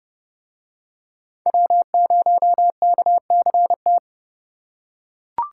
Morse code, heard as a keyed tone, about 28 wpm, 700 Hz. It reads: W0KCT